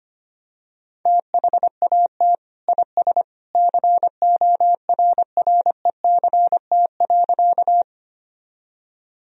Morse code, heard as a keyed tone, about 25 wpm, 700 Hz. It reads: THAT IS CORRECT.
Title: Sentences from Top 1000 Words - Review Part 2 - 25wpm